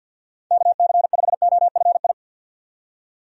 Morse code, heard as a keyed tone, about 46 wpm, 700 Hz. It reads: KC5OFI
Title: USA Call Signs - 6 Characters - 50wpm